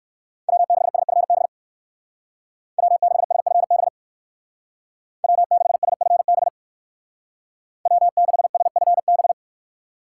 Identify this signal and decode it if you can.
Morse code, heard as a keyed tone, about 56 wpm, 700 Hz. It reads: W6SFB W6SFB W6SFB W6SFB
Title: Speed Racing - USA 1x3 Call Signs 45wpm